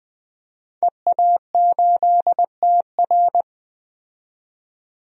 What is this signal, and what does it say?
Morse code, heard as a keyed tone, about 20 wpm, 700 Hz. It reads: EA8TR
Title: DXCC-100 Call Signs - 5 Characters - 20wpm